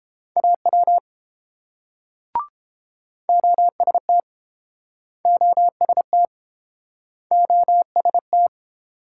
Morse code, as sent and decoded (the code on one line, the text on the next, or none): .- .-- / --- ... - / --- ... - / --- ... -
AW OST OST OST